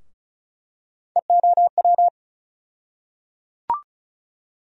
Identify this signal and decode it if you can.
Morse code, heard as a keyed tone, about 34 wpm, 700 Hz. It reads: EOW
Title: USA Call Sign Postfixes - 3 Letters 35wpm